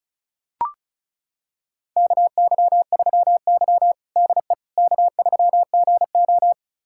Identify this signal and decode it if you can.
Morse code, heard as a keyed tone, about 36 wpm, 700 Hz. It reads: KY3Y DE K3GO
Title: USA Callsign DE Callsign - 4 Characters - 35wpm